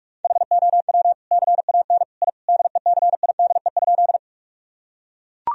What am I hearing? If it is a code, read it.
Morse code, heard as a keyed tone, about 41 wpm, 700 Hz. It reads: HOW CAN I DECIDE?
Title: Sentences from Top 1000 Words - Review Part 3 - 45wpm